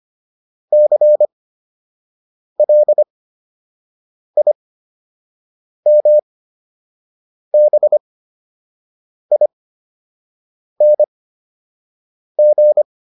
Morse code, heard as a keyed tone, about 25 wpm, spaced as for 6 wpm, 600 Hz. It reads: CLIMBING